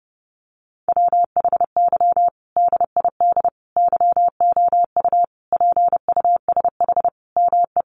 Morse code, heard as a keyed tone, about 30 wpm, 700 Hz. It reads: WHY DID YOU PUSH ME
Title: Sentences from Top 1300 Words 30wpm